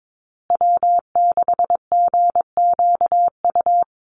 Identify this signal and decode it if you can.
Morse code, heard as a keyed tone, about 22 wpm, 700 Hz. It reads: W6GQU
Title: USA Call Signs Repeated 2x 22wpm